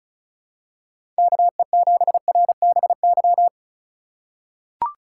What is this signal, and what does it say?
Morse code, heard as a keyed tone, about 36 wpm, 700 Hz. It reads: KE7RBY
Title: USA Call Signs - 6 Characters - 2x Character Spacing - 35wpm